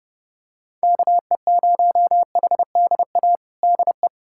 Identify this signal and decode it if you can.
Morse code, heard as a keyed tone, about 30 wpm, 700 Hz. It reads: KE0HDA DE
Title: USA Callsign DE Callsign - 6 Characters - 30wpm